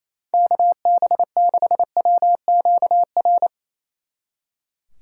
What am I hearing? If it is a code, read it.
Morse code, heard as a keyed tone, about 31 wpm, 700 Hz. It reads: KB6WQR
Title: Speed Racing - USA Call Signs 28wpm